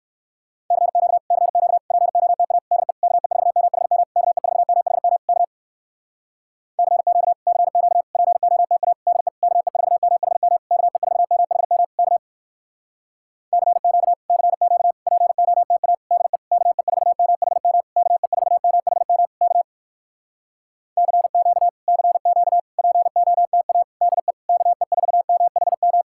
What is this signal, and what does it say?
Morse code, heard as a keyed tone, about 69 wpm, 700 Hz. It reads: CQ CQ POTA DE KE4MHM KE4MHM K CQ CQ POTA DE KE4MHM KE4MHM K CQ CQ POTA DE KE4MHM KE4MHM K CQ CQ POTA DE KE4MHM